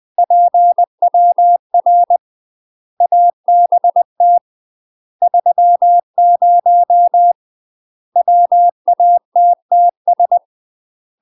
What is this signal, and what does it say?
Morse code, heard as a keyed tone, about 20 wpm, 700 Hz. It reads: PWR ABT 30 WATTS
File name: UK MASTER QSOs TRAINING / 20WPM / UK MASTER QSOs 0016